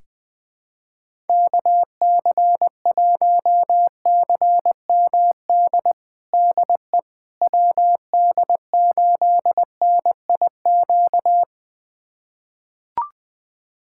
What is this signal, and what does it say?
Morse code, heard as a keyed tone, about 20 wpm, 700 Hz. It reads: KC1CMD DE WD8NIQ
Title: USA Callsign DE Callsign 20wpm